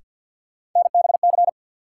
Morse code, heard as a keyed tone, about 47 wpm, 700 Hz. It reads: NBC
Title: Sets of 5 Words Encore - New Word Drill - 50wpm